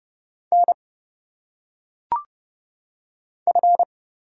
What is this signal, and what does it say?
Morse code, heard as a keyed tone, about 30 wpm, 700 Hz. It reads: N F
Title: Single Letters - Phonetic Pronunciation 30wpm